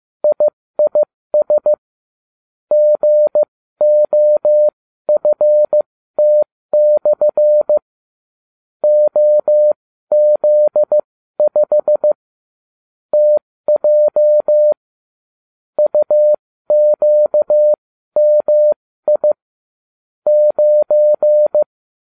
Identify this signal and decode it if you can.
Morse code, heard as a keyed tone, about 15 wpm, spaced as for 13 wpm, 600 Hz. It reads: IIS GOFT/ OZ5 TJ UQMI 9